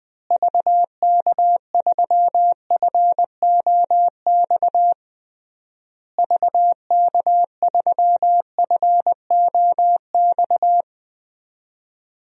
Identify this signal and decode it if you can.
Morse code, heard as a keyed tone, about 21 wpm, 700 Hz. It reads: VK3FOX VK3FOX
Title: DXCC-100 Call Signs - 6 Characters - Repeated 3x - 20wpm